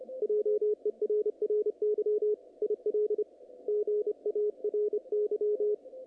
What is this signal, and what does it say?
Morse code, heard as a keyed tone, about 28 wpm, 440 Hz, with radio noise. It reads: JERRY IL GARY
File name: contests / naqp-cw-k5tr-op-k5pi- -2300